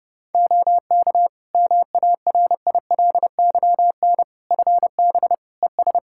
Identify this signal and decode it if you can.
Morse code, heard as a keyed tone, about 30 wpm, 700 Hz. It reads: OK MARILYN FB ES
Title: Practice QSO - Exchange 2 - Info - 30wpm